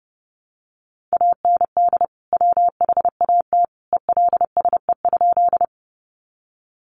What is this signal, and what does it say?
Morse code, heard as a keyed tone, about 30 wpm, 700 Hz. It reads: AND WHAT ELSE?
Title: Sentences from Top 800 Words 30wpm